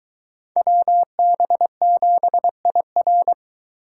Morse code, heard as a keyed tone, about 22 wpm, 700 Hz. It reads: WB7IR